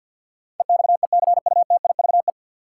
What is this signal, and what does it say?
Morse code, heard as a keyed tone, about 52 wpm, 700 Hz. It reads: EXECUTIVE
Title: Sets of 5 Words Encore - New Word Drill - 50wpm